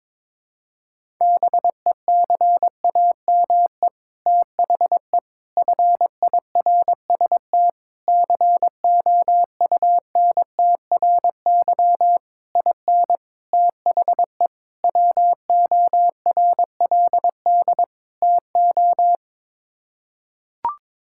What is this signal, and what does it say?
Morse code, heard as a keyed tone, about 22 wpm, 700 Hz. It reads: BECAME THE FIRST COUNTRY IN THE WORLD TO
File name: practice / 22wpm / Sets of 8 Words - 22wpm